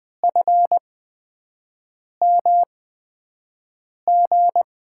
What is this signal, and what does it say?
Morse code, heard as a keyed tone, about 20 wpm, spaced as for 5 wpm, 700 Hz. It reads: FMG
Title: USA Call Signs - 6 Characters - 8x Character Spacing - 20wpm